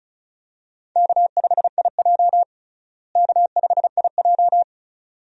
Morse code, heard as a keyed tone, about 36 wpm, 700 Hz. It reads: K5IJ K5IJ
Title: USA Call Signs - 4 Characters - Repeated 2x - 35wpm